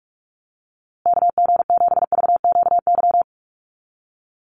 Morse code, heard as a keyed tone, about 44 wpm, 700 Hz. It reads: KG7VQY
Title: USA Call Signs Repeated 3x 45wpm